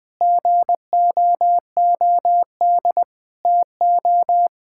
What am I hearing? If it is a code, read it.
Morse code, heard as a keyed tone, about 20 wpm, 700 Hz. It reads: GOOD TO
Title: Sentences from Top 1000 Words - Review Part 6 - 20wpm